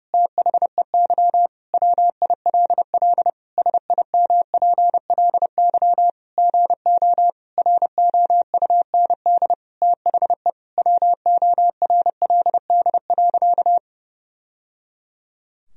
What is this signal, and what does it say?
Morse code, heard as a keyed tone, about 32 wpm, 700 Hz. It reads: THEY WILL SIMPLY GO ROUND THE WORLD.
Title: Sentences from Top 1000 Words - Review Part 4 - 30wpm